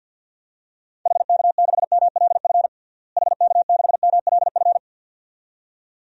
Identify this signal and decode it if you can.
Morse code, heard as a keyed tone, about 50 wpm, 700 Hz. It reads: HK6MLF HK6MLF
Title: DXCC-100 Call Signs - 6 Characters - Repeated 2x - 50wpm